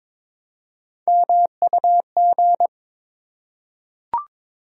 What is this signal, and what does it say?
Morse code, heard as a keyed tone, about 22 wpm, 700 Hz. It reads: MUG
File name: Difficult Words with Hints - 22wpm